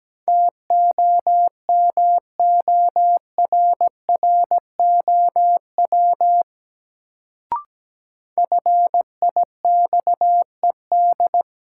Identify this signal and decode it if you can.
Morse code, heard as a keyed tone, about 17 wpm, 700 Hz. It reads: TOMORROW FIXED
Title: Sentences from Top 1100 Words - New Word Drill Alt - 17wpm